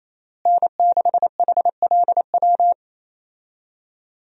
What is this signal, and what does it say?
Morse code, heard as a keyed tone, about 28 wpm, 700 Hz. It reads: N6HLW
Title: USA Call Signs - 5 Characters - Repeated 3x - 28wpm